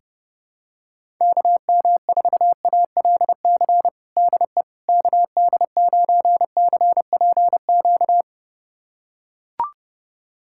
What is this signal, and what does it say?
Morse code, heard as a keyed tone, about 32 wpm, 700 Hz. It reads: KM4ALC DE KD9CPQ
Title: USA Callsign DE Callsign 30wpm